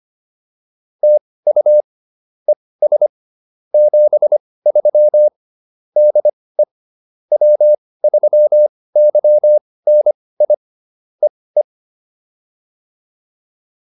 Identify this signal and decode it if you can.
Morse code, heard as a keyed tone, about 25 wpm, spaced as for 18 wpm, 600 Hz. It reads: TU ES 73 DE W3YNI EE
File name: POTA101 18